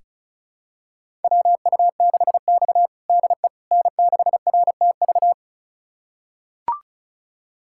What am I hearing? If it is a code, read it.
Morse code, heard as a keyed tone, about 36 wpm, 700 Hz. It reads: WU6X DE N6RTV
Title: USA Callsign DE Callsign 35wpm